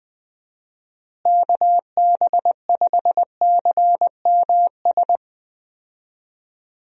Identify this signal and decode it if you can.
Morse code, heard as a keyed tone, about 21 wpm, 700 Hz. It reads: KB5CMS
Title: USA Call Signs - 6 Characters - Repeated 3x - 20wpm